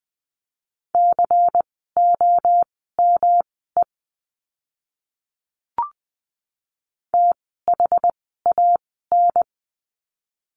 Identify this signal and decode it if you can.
Morse code, heard as a keyed tone, about 20 wpm, spaced as for 14 wpm, 700 Hz. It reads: COME THAN
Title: 185 Lesson 30f - Letter b - Words 20wpm